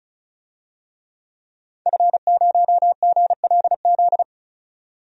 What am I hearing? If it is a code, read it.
Morse code, heard as a keyed tone, about 34 wpm, 700 Hz. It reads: F0GLZ
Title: DXCC-100 Call Signs - 5 Characters - 35wpm